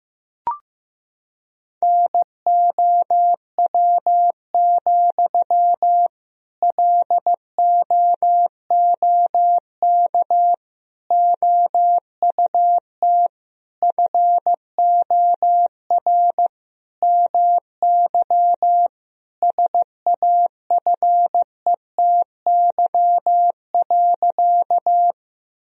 Morse code, heard as a keyed tone, about 15 wpm, 700 Hz. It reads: NOW, LOOK OUT FOR MY SAFETY.
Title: Sentences from Top 1000 Words - Review Part 4 - 15wpm